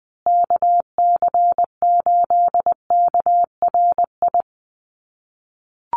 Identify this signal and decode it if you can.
Morse code, heard as a keyed tone, about 20 wpm, 700 Hz. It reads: KC8KRI
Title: USA Call Signs Repeated 3x 20wpm